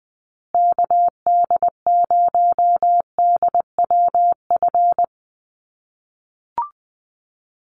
Morse code, heard as a keyed tone, about 20 wpm, 700 Hz. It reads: KD0DWF
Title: USA Call Signs Repeated 2x 20wpm